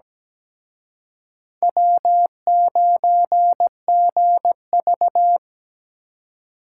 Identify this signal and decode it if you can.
Morse code, heard as a keyed tone, about 20 wpm, 700 Hz. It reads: W9GV
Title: Speed Racing - USA 1x2 Call Signs 17wpm